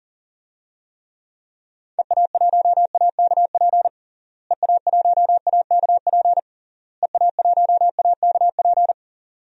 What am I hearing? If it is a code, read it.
Morse code, heard as a keyed tone, about 40 wpm, 700 Hz. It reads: EA1AKP EA1AKP EA1AKP
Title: DXCC-100 Call Signs - 6 Characters - Repeated 3x - 40wpm